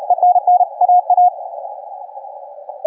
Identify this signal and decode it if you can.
Morse code, heard as a keyed tone, about 37 wpm, 720 Hz. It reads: ENN AA
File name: kf6mwa / cqww cw3